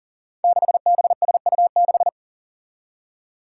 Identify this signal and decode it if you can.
Morse code, heard as a keyed tone, about 40 wpm, 700 Hz. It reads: 6BSU6
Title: Five Challenging Letter-Number Combinations 40wpm